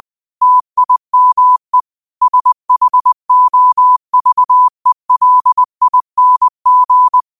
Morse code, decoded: TIME SHOVELING

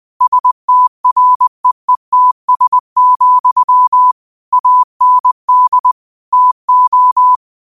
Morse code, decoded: STREETS, AND TO

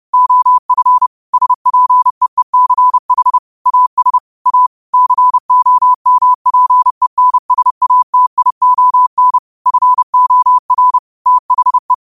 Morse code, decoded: OF SPEECH AS A COMPENSATION FOR THE

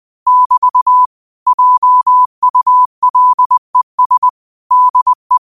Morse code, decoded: <BT> JULES DE

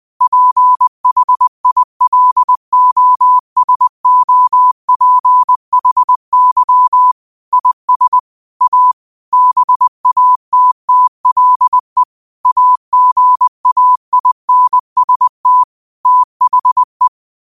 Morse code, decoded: PHILOSOPHY IS A BATTLE AGAINST THE

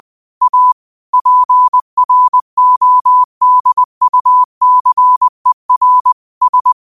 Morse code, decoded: A PRODUCER S

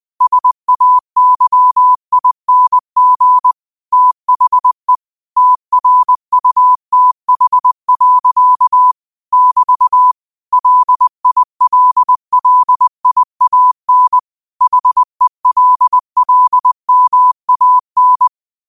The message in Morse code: SAYING THE TRUTH. <BT> LILLIAN HELLMAN

